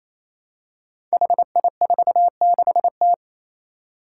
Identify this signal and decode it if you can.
Morse code, heard as a keyed tone, about 27 wpm, 700 Hz. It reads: HI46T
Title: Five Challenging Letter-Number Combinations 28wpm